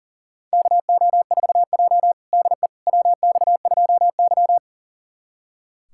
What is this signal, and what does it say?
Morse code, heard as a keyed tone, about 40 wpm, 700 Hz. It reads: KO4J DE WX2Y
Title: USA Callsign DE Callsign - 4 Characters - 40wpm